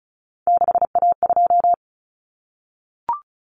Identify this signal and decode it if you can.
Morse code, heard as a keyed tone, about 34 wpm, 700 Hz. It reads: 6A2
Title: Common Cut Numbers - 3-Digit 35wpm